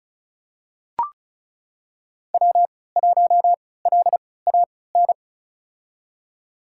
Morse code, heard as a keyed tone, about 36 wpm, spaced as for 20 wpm, 690 Hz. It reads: W1LAN
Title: USA Call Signs - 3x Character Spacing - 35wpm